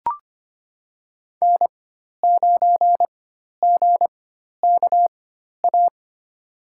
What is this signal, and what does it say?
Morse code, heard as a keyed tone, about 25 wpm, spaced as for 12 wpm, 700 Hz. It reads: N9GKA
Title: USA Call Signs - 5 Characters - 4x Character Spacing - 25wpm